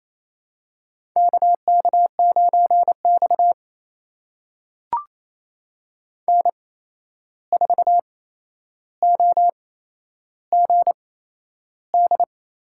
Morse code, heard as a keyed tone, about 28 wpm, spaced as for 11 wpm, 700 Hz. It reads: KK9X N4OGD
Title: USA Call Signs - 8x Character Spacing - 28wpm